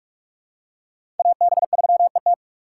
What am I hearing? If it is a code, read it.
Morse code, heard as a keyed tone, about 46 wpm, 700 Hz. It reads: AB3ET
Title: USA Call Signs - 5 Characters - 4x Character Spacing - 45wpm